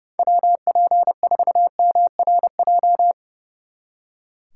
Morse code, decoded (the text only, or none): WP4MRJ